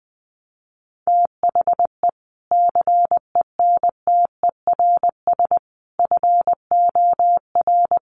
Morse code, heard as a keyed tone, about 20 wpm, 700 Hz. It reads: THE CENTERS FOR
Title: Sets of 5 Words Encore2 20wpm